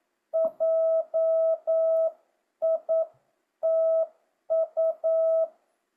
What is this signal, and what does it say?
Morse code, heard as a keyed tone, about 9 wpm, 650 Hz. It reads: JITU
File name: jh8cbh / 2025cw